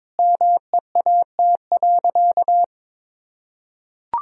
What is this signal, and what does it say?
Morse code, heard as a keyed tone, about 22 wpm, 700 Hz. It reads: MEAT.
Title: Sentences from Top 1000 Words - Review Part 3 - 22wpm